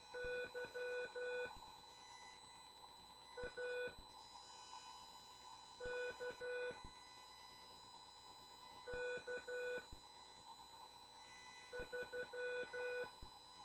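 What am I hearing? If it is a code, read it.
Morse code, heard as a keyed tone, about 12 wpm, 490 Hz. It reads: Y A K K 3
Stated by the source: CW class lesson from The Martian Net with instructor N2KPS.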